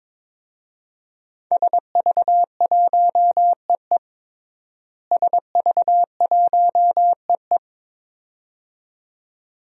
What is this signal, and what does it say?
Morse code, heard as a keyed tone, about 22 wpm, 700 Hz. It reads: SV1EE SV1EE
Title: DXCC-100 Call Signs - 5 Characters - Repeated 2x - 22wpm